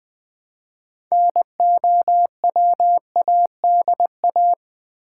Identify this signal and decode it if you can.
Morse code, heard as a keyed tone, about 20 wpm, 700 Hz. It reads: NOWADA
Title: Build A Word - 3 Letters At A Time - 8-Letter Words - 20wpm